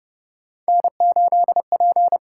Morse code, heard as a keyed tone, about 30 wpm, 700 Hz. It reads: N8P
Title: USA Call Signs - 5 Characters - 6x Character Spacing - 30wpm